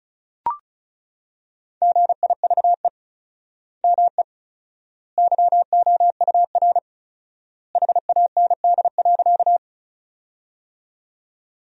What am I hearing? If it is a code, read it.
Morse code, heard as a keyed tone, about 36 wpm, 700 Hz. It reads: GIVE ME YOUR HAND.
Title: 4x Word-Spacing - Sentences from Top 200 Words - 35wpm